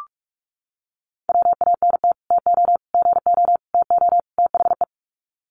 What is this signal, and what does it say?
Morse code, heard as a keyed tone, about 44 wpm, 700 Hz. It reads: WANT TO GO TO THE
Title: Sets of 5 Words Encore2 45wpm